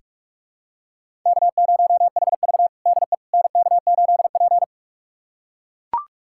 Morse code, heard as a keyed tone, about 41 wpm, 700 Hz. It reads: K0HV DE NK8P
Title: USA Callsign DE Callsign - 4 Characters - 45wpm